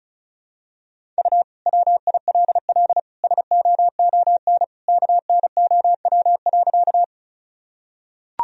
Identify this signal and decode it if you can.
Morse code, heard as a keyed tone, about 36 wpm, 700 Hz. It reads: U WILL SOON KNOW.
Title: Sentences from Top 400 Words and Abbreviations 35wpm